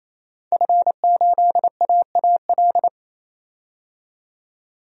Morse code, heard as a keyed tone, about 28 wpm, 700 Hz. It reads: F8AAL
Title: DXCC-100 Call Signs - 5 Characters - 28wpm